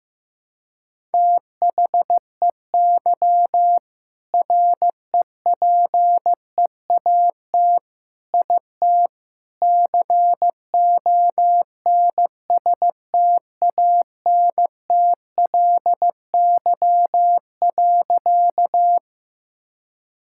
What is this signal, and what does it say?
Morse code, heard as a keyed tone, about 15 wpm, 700 Hz. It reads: THEY REPEAT IT CONSTANTLY.
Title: Sentences from Top 1600 Words 15wpm